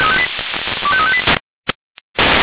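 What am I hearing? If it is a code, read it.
Five-tone selective call (ZVEI), heard as quick runs of five tones, tones of about 70 ms. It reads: 25358
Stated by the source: Received on an SDR.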